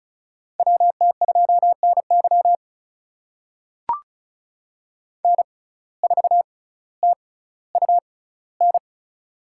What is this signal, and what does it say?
Morse code, heard as a keyed tone, about 34 wpm, spaced as for 18 wpm, 700 Hz. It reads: WT2NY N4TUN
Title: USA Call Signs - 5 Characters - 6x Character Spacing - 35wpm